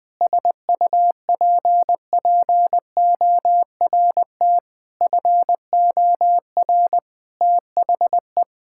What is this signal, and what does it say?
Morse code, heard as a keyed tone, about 20 wpm, 700 Hz. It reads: SUPPORT FOR THE
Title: Sentences from Top 1000 Words - Review Part 5 - 20wpm